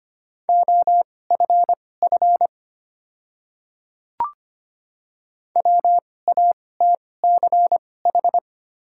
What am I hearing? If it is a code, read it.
Morse code, heard as a keyed tone, about 25 wpm, spaced as for 18 wpm, 690 Hz. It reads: OFF WATCH
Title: 2x Character Spacing - Top 300 Words - 25wpm